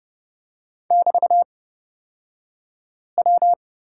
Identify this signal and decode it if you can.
Morse code, heard as a keyed tone, about 30 wpm, spaced as for 5 wpm, 700 Hz. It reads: <BT>W
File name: Lesson 36d - BT - Two Characters 30wpm